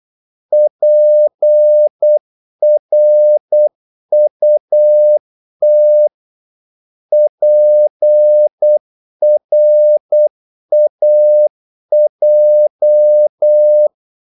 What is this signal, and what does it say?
Morse code, heard as a keyed tone, about 8 wpm, 600 Hz. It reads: PRUT PRAJ
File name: Korte ord DK 8wpm